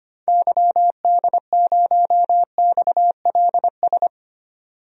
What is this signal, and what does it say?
Morse code, heard as a keyed tone, about 25 wpm, 690 Hz. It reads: YD0XLS
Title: DXCC-100 Call Signs - 6 Characters - 25wpm